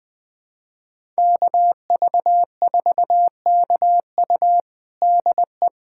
Morse code, heard as a keyed tone, about 20 wpm, 700 Hz. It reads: KV4KU DE
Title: USA Callsign DE Callsign 20wpm